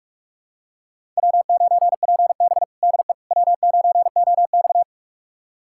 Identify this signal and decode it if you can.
Morse code, heard as a keyed tone, about 47 wpm, 700 Hz. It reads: W9PB DE W9OX
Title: USA Callsign DE Callsign - 4 Characters - 45wpm